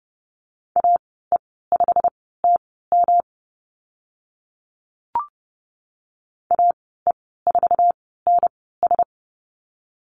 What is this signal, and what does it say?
Morse code, heard as a keyed tone, about 30 wpm, spaced as for 17 wpm, 700 Hz. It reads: AE5TM AE4NS
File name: Lesson 19g - Letter w - Call Signs 30wpm